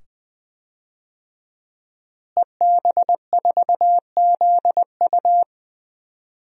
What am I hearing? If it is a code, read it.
Morse code, heard as a keyed tone, about 19 wpm, 700 Hz. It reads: EB4ZU
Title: DXCC-100 Call Signs - 5 Characters - Repeated 2x - 20wpm